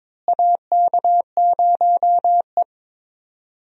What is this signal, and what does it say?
Morse code, heard as a keyed tone, about 22 wpm, 700 Hz. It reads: AK0E
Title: USA Call Signs - 4 Characters - Repeated 3x - 22wpm